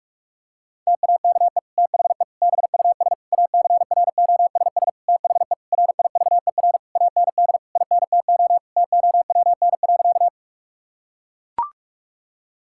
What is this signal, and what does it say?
Morse code, heard as a keyed tone, about 41 wpm, 700 Hz. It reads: TAKE THE BUS ACROSS THE RIVER AND INTO TOWN.
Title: Sentences from Top 1000 Words - Review Part 6 - 45wpm